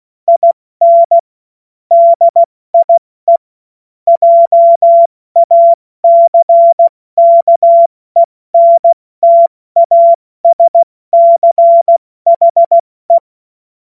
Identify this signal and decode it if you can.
Morse code, heard as a keyed tone, about 16 wpm, 680 Hz. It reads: IN DIE JACKENTASCHE
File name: Morsekurs / Witze / Witz 05. April 2022 70 F80